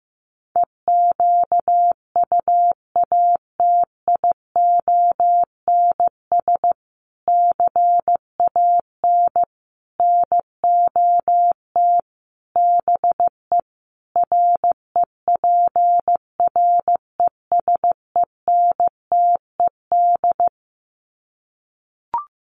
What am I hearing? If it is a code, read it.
Morse code, heard as a keyed tone, about 15 wpm, 700 Hz. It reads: EQUATIONS CAN NOT BE REPRESENTED
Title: Sets of 5 Words 15wpm